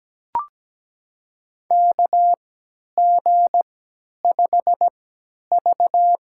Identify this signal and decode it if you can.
Morse code, heard as a keyed tone, about 17 wpm, spaced as for 10 wpm, 700 Hz. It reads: KG5V
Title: USA Call Signs - 3x Character Spacing - 17wpm